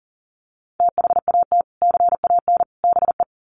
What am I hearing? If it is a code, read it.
Morse code, heard as a keyed tone, about 40 wpm, 700 Hz. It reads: THAT CAN BE
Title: Sets of 3 Words 40wpm